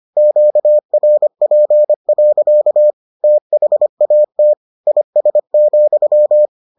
Morse code, heard as a keyed tone, about 25 wpm, 600 Hz. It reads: QRP. THAT IS,